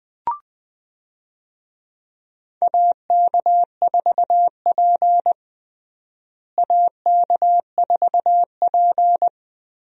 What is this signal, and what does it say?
Morse code, heard as a keyed tone, about 20 wpm, 700 Hz. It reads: AK4P AK4P
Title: USA Call Signs - 4 Characters - Repeated 2x - 20wpm